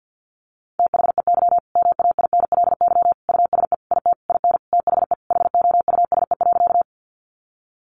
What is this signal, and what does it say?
Morse code, heard as a keyed tone, about 50 wpm, 700 Hz. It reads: THEY MAINLY USE IT IN THE HOUSE.
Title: Sentences from Top 1400 Words 50wpm